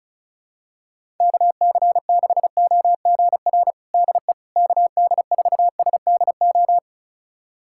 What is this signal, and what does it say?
Morse code, heard as a keyed tone, about 34 wpm, 700 Hz. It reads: KC6OGR DE KD4SDO
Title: USA Callsign DE Callsign 35wpm